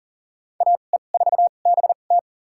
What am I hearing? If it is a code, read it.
Morse code, heard as a keyed tone, about 37 wpm, spaced as for 28 wpm, 700 Hz. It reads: AE4BT
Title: USA Call Signs - 5 Characters - 2x Character Spacing - 40wpm